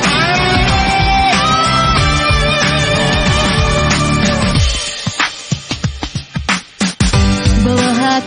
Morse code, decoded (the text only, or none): none